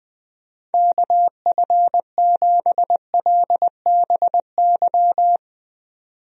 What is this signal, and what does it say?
Morse code, heard as a keyed tone, about 20 wpm, 700 Hz. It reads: KF7LBY
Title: USA Call Signs - 6 Characters - 20wpm